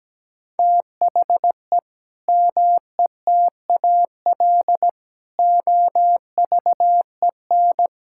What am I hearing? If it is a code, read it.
Morse code, heard as a keyed tone, about 17 wpm, 700 Hz. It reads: THE METAL OVEN
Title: Sentences from Top 1000 Words - Review Part 6 - 17wpm